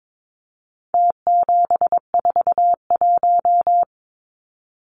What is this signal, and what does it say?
Morse code, heard as a keyed tone, about 22 wpm, 700 Hz. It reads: T741
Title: Numbers - Contest Style - 4-Digit 22wpm